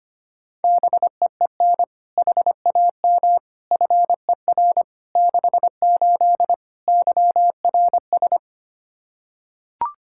Morse code, heard as a keyed tone, about 25 wpm, 700 Hz. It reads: BEEN HAM FER 68 YRS
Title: Practice QSO - Exchange 3 - More Info - 25wpm